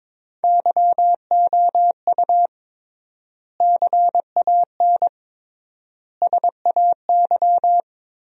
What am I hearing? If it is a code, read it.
Morse code, heard as a keyed tone, about 21 wpm, 700 Hz. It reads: YOU CAN SAY
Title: 3x Word-Spacing - Sentences from Top 200 Words - 22wpm